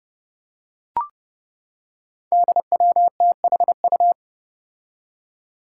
Morse code, heard as a keyed tone, about 30 wpm, 700 Hz. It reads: DWTHU
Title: Five Challenging Letter-Number Combinations 30wpm